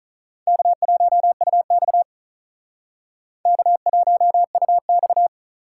Morse code, heard as a keyed tone, about 39 wpm, 700 Hz. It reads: K1UX K1UX
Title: Speed Racing - USA 1x2 Call Signs 35wpm